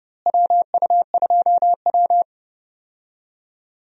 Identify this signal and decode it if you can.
Morse code, heard as a keyed tone, about 30 wpm, 700 Hz. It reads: WU2W
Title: USA Call Signs - 4 Characters - Repeated 3x - 30wpm